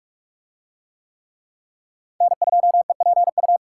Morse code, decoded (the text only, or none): N2EPU